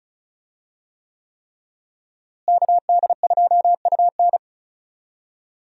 Morse code, heard as a keyed tone, about 36 wpm, 700 Hz. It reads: KD2UN
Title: USA Call Signs - 5 Characters - Repeated 3x - 35wpm